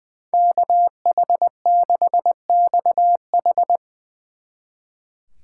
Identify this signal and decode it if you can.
Morse code, heard as a keyed tone, about 21 wpm, 700 Hz. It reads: KH6XH
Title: Speed Racing - USA 2x2 Call Signs 20wpm